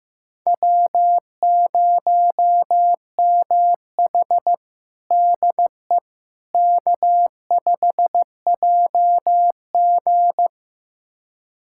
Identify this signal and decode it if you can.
Morse code, heard as a keyed tone, about 15 wpm, 700 Hz. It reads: W0MH DE K5JG
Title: USA Callsign DE Callsign - 4 Characters - 15wpm